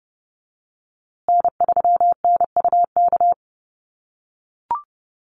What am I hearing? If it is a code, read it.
Morse code, heard as a keyed tone, about 30 wpm, 700 Hz. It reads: N3NUK
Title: USA Call Signs Repeated 2x 30wpm